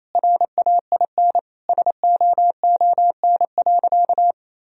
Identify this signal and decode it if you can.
Morse code, heard as a keyed tone, about 28 wpm, 700 Hz. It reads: RAIN SOON.
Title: Sentences from Top 1000 Words - Review Part 1 - 28wpm